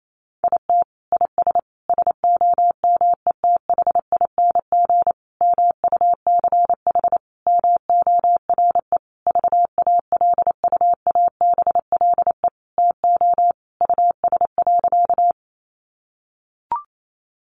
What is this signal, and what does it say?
Morse code, heard as a keyed tone, about 28 wpm, 700 Hz. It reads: IT IS SOMETHING MUCH MORE VALUABLE TO US.
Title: Sentences from Top 1100 Words 28wpm